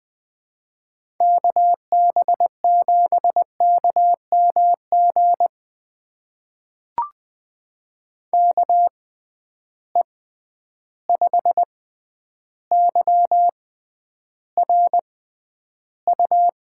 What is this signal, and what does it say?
Morse code, heard as a keyed tone, about 20 wpm, spaced as for 10 wpm, 700 Hz. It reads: KB7KMG KE5YRU
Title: USA Call Signs - 6 Characters - 6x Character Spacing - 20wpm